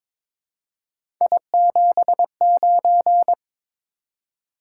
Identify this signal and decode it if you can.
Morse code, heard as a keyed tone, about 23 wpm, 700 Hz. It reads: I79
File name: practice / 22wpm / Three Letter-Number Combinations - 22wpm